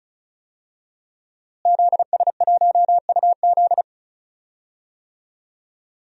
Morse code, heard as a keyed tone, about 36 wpm, 700 Hz. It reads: ZS1UZ
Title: DXCC-100 Call Signs - 5 Characters - Repeated 2x - 35wpm